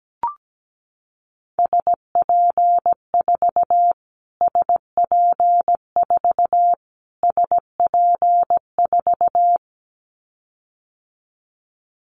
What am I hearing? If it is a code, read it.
Morse code, heard as a keyed tone, about 17 wpm, 700 Hz. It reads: SP4 SP4 SP4
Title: Callsign Prefixes for Top 25 DXCC - Repeated 3x - 17wpm